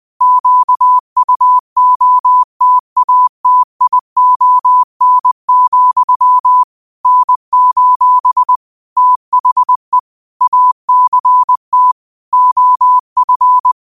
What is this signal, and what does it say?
Morse code, heard as a keyed tone, about 20 wpm, 1000 Hz. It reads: QUOTATION, N: THE ACT OF
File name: qotdfiles / 20wpm